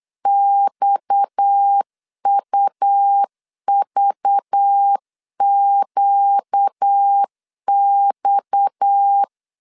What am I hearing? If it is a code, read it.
Morse code, heard as a keyed tone, about 8 wpm, 800 Hz. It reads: XUVQX
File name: LEZIONI / 6 PKX0 / 6 PKX 8wpm 3041